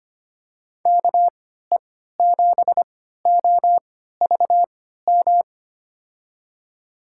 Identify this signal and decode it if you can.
Morse code, heard as a keyed tone, about 25 wpm, spaced as for 14 wpm, 700 Hz. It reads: KE7OVM
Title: USA Call Signs - 6 Characters - 3x Character Spacing - 25wpm